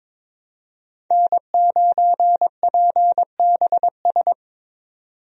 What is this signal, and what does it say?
Morse code, heard as a keyed tone, about 22 wpm, 700 Hz. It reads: N9PBS
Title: USA Call Signs - 5 Characters - 22wpm